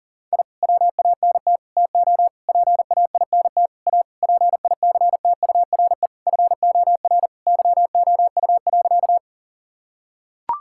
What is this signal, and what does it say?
Morse code, heard as a keyed tone, about 40 wpm, 700 Hz. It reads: I WANT TO PAINT A PICTURE FOR YOU.
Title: Sentences from Top 1000 Words - Review Part 6 - 40wpm